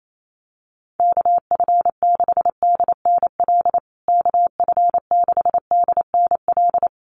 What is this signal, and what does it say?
Morse code, heard as a keyed tone, about 28 wpm, 700 Hz. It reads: KF6DNL KF6DNL
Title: USA Call Signs Repeated 3x 28wpm